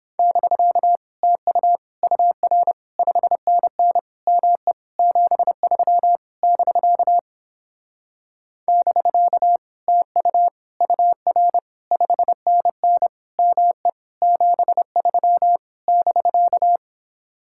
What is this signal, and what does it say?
Morse code, heard as a keyed tone, about 29 wpm, 700 Hz. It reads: <BK> TU UR 5NN ME 73 <BK> <BK> TU UR 5NN ME 73 <BK>